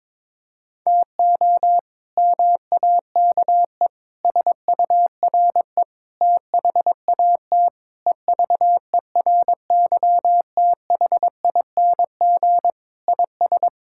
Morse code, heard as a keyed tone, about 22 wpm, 700 Hz. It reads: TO MAKE SURE THAT EVERYTHING IS
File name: practice / 22wpm / Sentences with 4-Word Phrases - 22wpm